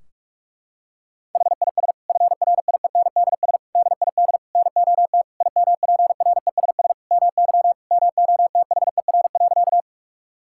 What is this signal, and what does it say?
Morse code, heard as a keyed tone, about 41 wpm, 700 Hz. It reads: HIS FRIENDS DID NOT IMPRESS MY MOTHER.
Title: Sentences from Top 1600 Words 45wpm